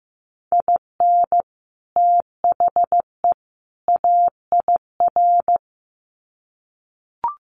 Morse code, heard as a keyed tone, about 15 wpm, 700 Hz. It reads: IN THE AIR
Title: Sets of 3 Words 15wpm